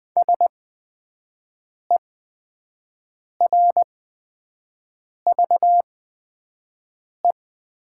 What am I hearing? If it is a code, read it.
Morse code, heard as a keyed tone, about 20 wpm, spaced as for 5 wpm, 700 Hz. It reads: SERVE
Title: 8x Character Spacing - Top 500 Words - 20wpm